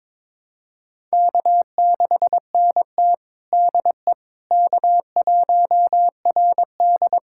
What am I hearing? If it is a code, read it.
Morse code, heard as a keyed tone, about 22 wpm, 700 Hz. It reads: K6NT DE K1RD
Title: USA Callsign DE Callsign - 4 Characters - 22wpm